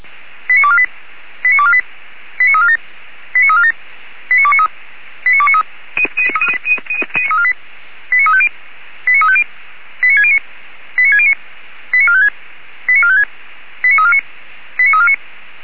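Five-tone selective call (ZVEI), heard as quick runs of five tones, tones of about 70 ms. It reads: 89248 89248 89347 89347 89293 89293 89350 89350 89609 89609 89456 89456 89349 89349
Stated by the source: ZVEI five-tone selective call 89360 received by SDR.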